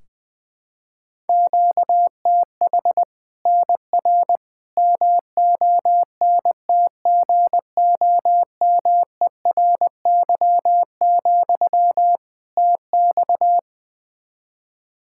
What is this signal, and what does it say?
Morse code, heard as a keyed tone, about 21 wpm, 700 Hz. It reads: QTH NR MONTGOMERY, TX
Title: Practice QSO - Exchange 1 - Beginning - 20wpm